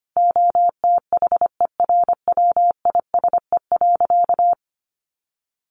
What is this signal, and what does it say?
Morse code, heard as a keyed tone, about 25 wpm, 700 Hz. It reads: OTHERWISE.
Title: Sentences from Top 1200 Words 25wpm